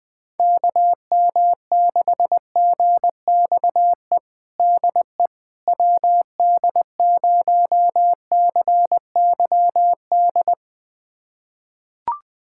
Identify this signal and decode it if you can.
Morse code, heard as a keyed tone, about 20 wpm, 700 Hz. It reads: KM6GXE DE WD0CYD
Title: USA Callsign DE Callsign - 6 Characters - 20wpm